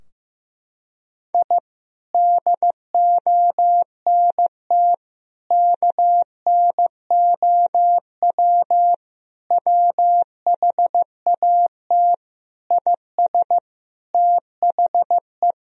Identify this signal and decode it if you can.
Morse code, heard as a keyed tone, about 16 wpm, 700 Hz. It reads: I DONT KNOW WHAT IS THE
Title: Sentences from Top 1000 Words - Review Part 4 - 15wpm